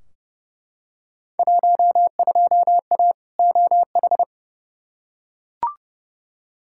Morse code, heard as a keyed tone, about 31 wpm, 700 Hz. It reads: 12A OH
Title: Field Day Exchanges Encore - 30wpm